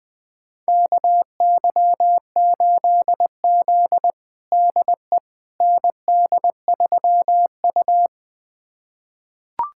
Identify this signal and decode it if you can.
Morse code, heard as a keyed tone, about 20 wpm, 700 Hz. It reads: KY8Z DE ND3U
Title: USA Callsign DE Callsign - 4 Characters - 20wpm